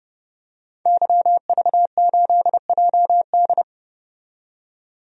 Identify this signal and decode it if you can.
Morse code, heard as a keyed tone, about 31 wpm, 700 Hz. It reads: YV8JD
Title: DXCC-100 Call Signs - 5 Characters - 30wpm